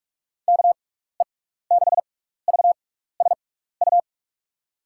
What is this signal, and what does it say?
Morse code, heard as a keyed tone, about 41 wpm, spaced as for 15 wpm, 700 Hz. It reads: KE6VSU
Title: USA Call Signs - 6x Character Spacing - 45wpm